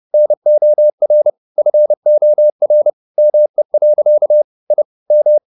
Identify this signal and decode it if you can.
Morse code, heard as a keyed tone, about 30 wpm, 600 Hz. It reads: NOR FOR ME. I M